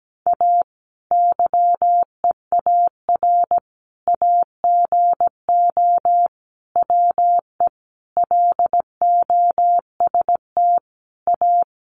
Morse code, heard as a keyed tone, about 17 wpm, 700 Hz. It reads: A YEAR AGO WE LOST A
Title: Sentences from Top 1400 Words 17wpm